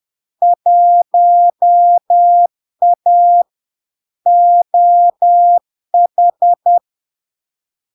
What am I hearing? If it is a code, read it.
Morse code, heard as a keyed tone, about 10 wpm, 700 Hz. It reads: 1A OH
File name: LARK-FD-2022-Log-in-CW-10wpm-700-Hz